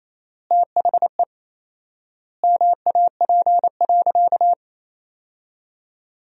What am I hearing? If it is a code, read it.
Morse code, heard as a keyed tone, about 28 wpm, 700 Hz. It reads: THE MAP.
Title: 4x Word-Spacing - Sentences from Top 400 Words - 28wpm